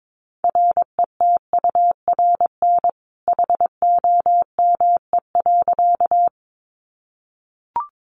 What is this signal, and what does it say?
Morse code, heard as a keyed tone, about 22 wpm, 700 Hz. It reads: RETURN HOME.
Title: Sentences from Top 800 Words 22wpm